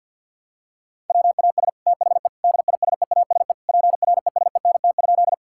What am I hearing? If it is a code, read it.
Morse code, heard as a keyed tone, about 52 wpm, 700 Hz. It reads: WAS THE DISEASE PRESENT?
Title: Sentences from Top 1000 Words - Review Part 6 - 50wpm